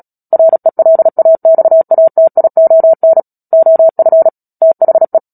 Morse code, heard as a keyed tone, about 37 wpm, 650 Hz. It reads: RELAXATION OF THE
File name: Book Two / 10-Chapter 10 - Epilogue - 36 WPM